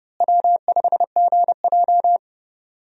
Morse code, heard as a keyed tone, about 30 wpm, 700 Hz. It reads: W5GJ
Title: USA Call Signs - 4 Characters - Repeated 3x - 30wpm